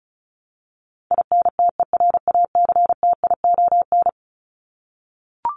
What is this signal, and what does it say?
Morse code, heard as a keyed tone, about 34 wpm, 700 Hz. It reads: INTERACTION
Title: Top 1251-1500 Words 35wpm